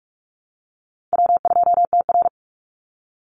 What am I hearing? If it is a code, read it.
Morse code, heard as a keyed tone, about 44 wpm, 700 Hz. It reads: W2TR
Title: Easy Call Signs Repeated 3x - 45wpm